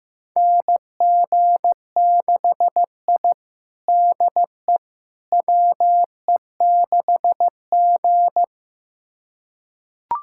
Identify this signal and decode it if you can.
Morse code, heard as a keyed tone, about 15 wpm, 700 Hz. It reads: NG6I DE WE6G
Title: USA Callsign DE Callsign - 4 Characters - 15wpm